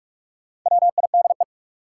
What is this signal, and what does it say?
Morse code, heard as a keyed tone, about 44 wpm, 700 Hz. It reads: WIDE
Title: Sets of 4 Words - New Word Drill - 45wpm